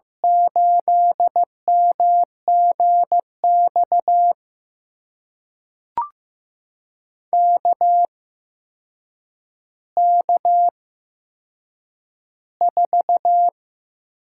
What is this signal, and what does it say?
Morse code, heard as a keyed tone, about 15 wpm, spaced as for 7 wpm, 700 Hz. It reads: 8MGX KK4
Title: USA Call Signs - 6 Characters - 8x Character Spacing - 15wpm